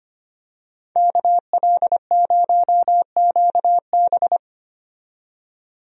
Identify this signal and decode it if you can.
Morse code, heard as a keyed tone, about 25 wpm, 700 Hz. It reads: KL0QB
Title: USA Call Signs - 5 Characters - 25wpm